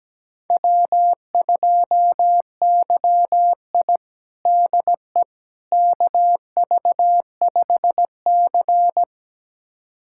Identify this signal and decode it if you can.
Morse code, heard as a keyed tone, about 17 wpm, 700 Hz. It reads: W2YI DE KV5C
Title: USA Callsign DE Callsign - 4 Characters - 17wpm